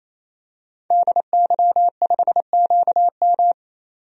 Morse code, heard as a keyed tone, about 28 wpm, 700 Hz. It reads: DY5QM
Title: DXCC-100 Call Signs - 5 Characters - 28wpm